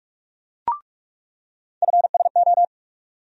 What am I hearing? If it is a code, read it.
Morse code, heard as a keyed tone, about 46 wpm, 700 Hz. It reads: FSO
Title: USA Call Sign Postfixes - 3 Letters 45wpm